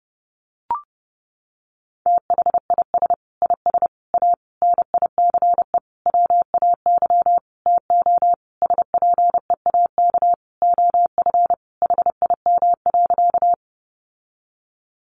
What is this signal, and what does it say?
Morse code, heard as a keyed tone, about 30 wpm, 700 Hz. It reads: THIS IS A NICE WAY TO SPEAK OF HIM.
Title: Sentences from Top 1100 Words 30wpm